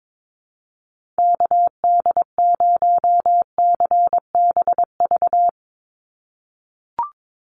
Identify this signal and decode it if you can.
Morse code, heard as a keyed tone, about 22 wpm, 700 Hz. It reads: KD0CBV
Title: USA Call Signs 22wpm